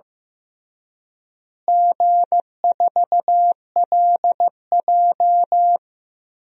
Morse code, heard as a keyed tone, about 16 wpm, 700 Hz. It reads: G4LJ
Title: DXCC-100 Call Signs - 4 Characters - Repeated 3x - 15wpm